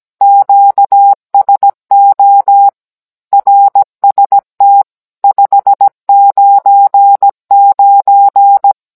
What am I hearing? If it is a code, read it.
Morse code, heard as a keyed tone, about 17 wpm, 800 Hz. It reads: QSO RST 599
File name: Cours-NR19-16mm